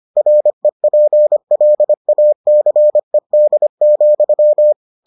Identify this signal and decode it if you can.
Morse code, heard as a keyed tone, about 25 wpm, 600 Hz. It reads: REPLACED,